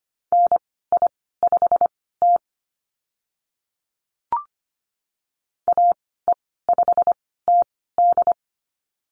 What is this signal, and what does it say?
Morse code, heard as a keyed tone, about 25 wpm, spaced as for 16 wpm, 700 Hz. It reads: NI5T AE5TD
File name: Lesson 15g - Number 5 - Call Signs 25wpm